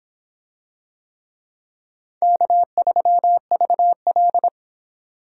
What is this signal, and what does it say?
Morse code, heard as a keyed tone, about 26 wpm, 700 Hz. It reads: K3VL